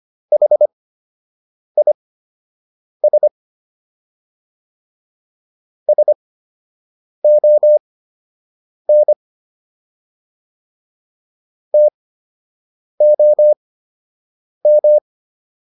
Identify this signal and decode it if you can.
Morse code, heard as a keyed tone, about 25 wpm, spaced as for 7 wpm, 600 Hz. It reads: HIS SON TOM